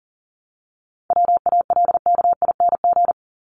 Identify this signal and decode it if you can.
Morse code, heard as a keyed tone, about 40 wpm, 700 Hz. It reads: WALKING
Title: ING Suffix - Top 100 Words with Base Words 40wpm